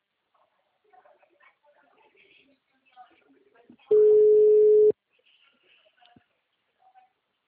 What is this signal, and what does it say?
Ringback tone (Europe, 425 Hz), with speech.